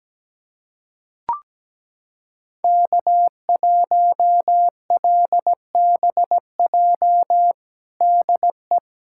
Morse code, heard as a keyed tone, about 17 wpm, 700 Hz. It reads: K1LBJ DE
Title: USA Callsign DE Callsign - 5 Characters - 17wpm